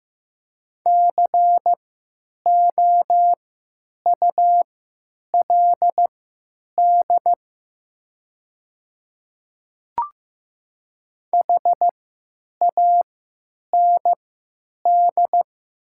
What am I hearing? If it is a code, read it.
Morse code, heard as a keyed tone, about 15 wpm, spaced as for 9 wpm, 700 Hz. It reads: COULD HAND
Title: 3x Character Spacing - Top 200 Words - 15wpm